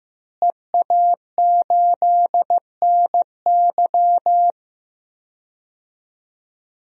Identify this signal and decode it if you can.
Morse code, heard as a keyed tone, about 15 wpm, 700 Hz. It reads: EA8NY
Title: DXCC-100 Call Signs - 5 Characters - Repeated 2x - 15wpm